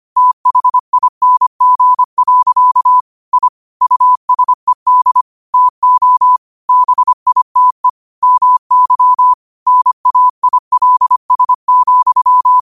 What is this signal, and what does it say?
Morse code, heard as a keyed tone, about 25 wpm, 1000 Hz. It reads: THING. I USED TO BITE MY NAILS,
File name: qotdfiles / 25wpm